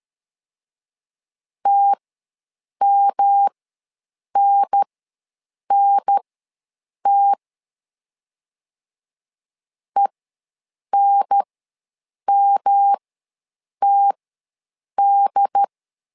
Morse code, heard as a keyed tone, about 13 wpm, spaced as for 7 wpm, 800 Hz. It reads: TMNNT ENMTD
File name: LEZIONI / 3 MWRO / 3 M ws 12wpm 1745